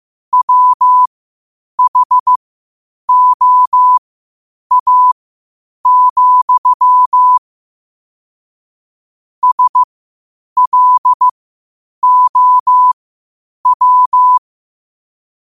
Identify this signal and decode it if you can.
Morse code, heard as a keyed tone, about 15 wpm, spaced as for 8 wpm, 1000 Hz. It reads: WHOA, SLOW
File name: qotdfiles / 8wpm